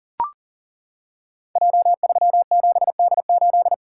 Morse code, heard as a keyed tone, about 38 wpm, 700 Hz. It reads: J37D8
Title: Code Groups - 5 Letter:Numbers - 40wpm